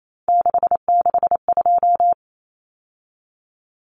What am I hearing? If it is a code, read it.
Morse code, heard as a keyed tone, about 28 wpm, 700 Hz. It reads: 662
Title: Numbers - 3-Digit 28wpm